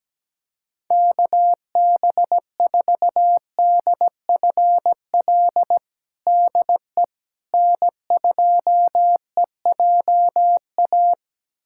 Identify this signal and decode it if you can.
Morse code, heard as a keyed tone, about 17 wpm, 700 Hz. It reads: KB4DFL DE N2EJA
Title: USA Callsign DE Callsign 17wpm